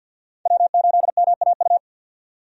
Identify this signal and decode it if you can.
Morse code, heard as a keyed tone, about 53 wpm, 700 Hz. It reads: W8MAU
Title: USA Call Signs - 5 Characters - 50wpm